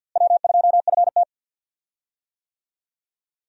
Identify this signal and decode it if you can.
Morse code, heard as a keyed tone, about 48 wpm, 700 Hz. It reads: W2FT
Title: USA Call Signs - 4 Characters - Repeated 3x - 50wpm